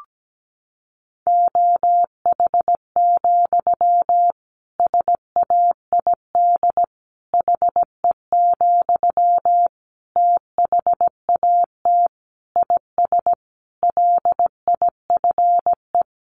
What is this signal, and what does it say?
Morse code, heard as a keyed tone, about 17 wpm, 700 Hz. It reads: OH, SAID HE, THAT IS LIFE
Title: Sentences from Top 500 Words 17wpm